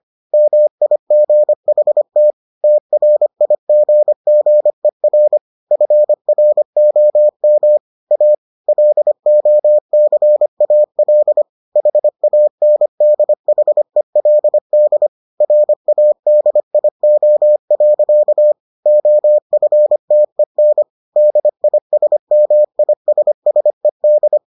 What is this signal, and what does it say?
Morse code, heard as a keyed tone, about 25 wpm, 600 Hz. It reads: MIGHT TRIGGER FROM A LOCAL HANDHELD RADIO. OFTEN DISMISSED